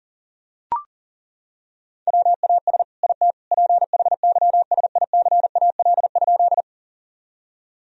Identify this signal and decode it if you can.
Morse code, heard as a keyed tone, about 40 wpm, 700 Hz. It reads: WAS IT PHYSICAL?
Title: Sentences from Top 1000 Words - Review Part 4 - 40wpm